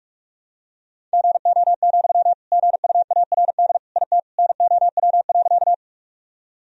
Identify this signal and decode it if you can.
Morse code, heard as a keyed tone, about 41 wpm, 700 Hz. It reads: GO, GUARD IT NOW.
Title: Sentences from Top 1000 Words - Review Part 4 - 45wpm